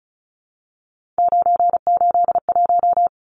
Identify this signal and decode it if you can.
Morse code, heard as a keyed tone, about 34 wpm, 700 Hz. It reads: 981
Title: Numbers - 3-Digit 35wpm